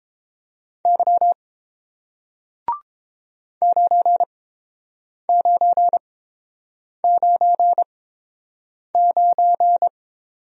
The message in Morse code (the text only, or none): Y 9999